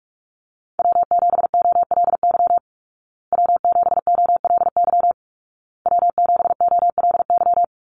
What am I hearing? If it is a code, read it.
Morse code, heard as a keyed tone, about 44 wpm, 700 Hz. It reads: W7OLY W7OLY W7OLY
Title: Call Signs Repeated 2x - 45wpm